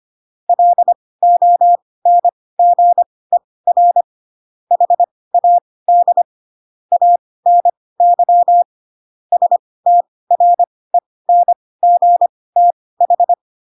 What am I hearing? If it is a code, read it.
Morse code, heard as a keyed tone, about 25 wpm, spaced as for 18 wpm, 700 Hz. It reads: LONGER HAD ANY STRENGTH
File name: Pinocchio / Pinocchio ch6 12 25 ik1hge